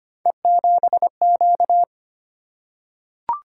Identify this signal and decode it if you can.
Morse code, heard as a keyed tone, about 25 wpm, 700 Hz. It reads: E7Q
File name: Three Letter-Number Combinations - 25wpm